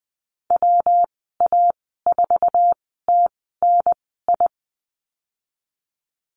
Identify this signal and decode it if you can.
Morse code, heard as a keyed tone, about 20 wpm, spaced as for 14 wpm, 700 Hz. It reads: WA4TNI
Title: 161 Lesson 26g - Letter g - Call Signs 20wpm